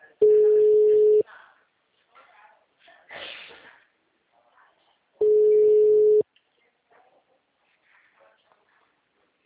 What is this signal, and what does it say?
Ringback tone (Europe, 425 Hz), with speech.